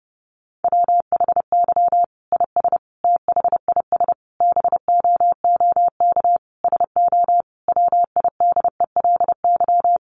Morse code, heard as a keyed tone, about 30 wpm, 700 Hz. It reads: WHY IS THIS BOOK SO WIDELY
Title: Sentences from Top 1200 Words 30wpm